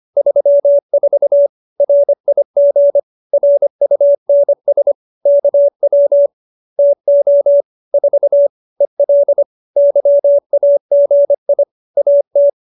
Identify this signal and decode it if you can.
Morse code, heard as a keyed tone, about 25 wpm, 570 Hz. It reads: 34 RIG RUNS KW TO 4 EL YAGI AT